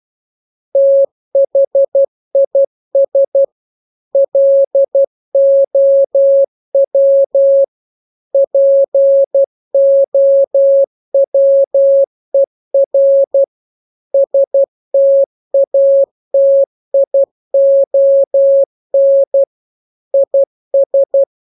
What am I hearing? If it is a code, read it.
Morse code, heard as a keyed tone, about 12 wpm, 550 Hz. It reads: THIS LOW POWER STATION IS
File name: jokes / joke 4 12 0000